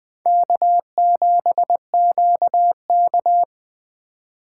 Morse code, heard as a keyed tone, about 20 wpm, 700 Hz. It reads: K7QK